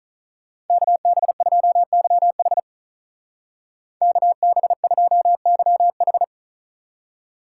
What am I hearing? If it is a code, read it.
Morse code, heard as a keyed tone, about 43 wpm, 700 Hz. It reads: KB2YH KB2YH
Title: Speed Racing - USA 2x2 Call Signs 35wpm